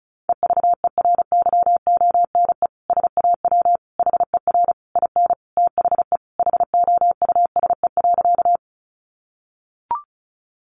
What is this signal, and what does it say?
Morse code, heard as a keyed tone, about 34 wpm, 700 Hz. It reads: EVERYONE SAW HER IN THE HOUSE.
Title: Sentences from Top 800 Words 35wpm